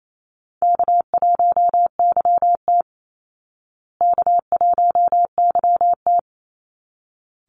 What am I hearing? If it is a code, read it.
Morse code, heard as a keyed tone, about 28 wpm, 690 Hz. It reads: K1YT K1YT
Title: Call Signs Repeated 3x - 28wpm